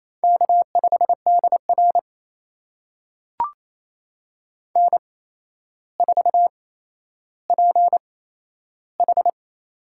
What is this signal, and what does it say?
Morse code, heard as a keyed tone, about 28 wpm, spaced as for 12 wpm, 700 Hz. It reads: K5DR N4PH
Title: USA Call Signs - 4 Characters - 8x Character Spacing - 28wpm